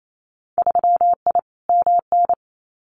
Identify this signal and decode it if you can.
Morse code, heard as a keyed tone, about 28 wpm, 700 Hz. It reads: 3I MN
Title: Winter Field Day Exchanges 28wpm